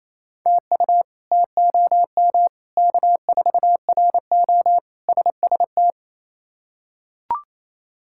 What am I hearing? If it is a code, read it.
Morse code, heard as a keyed tone, about 28 wpm, 700 Hz. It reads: TU TOM K4RO SST
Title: K1USN SST QSO Practice - 28wpm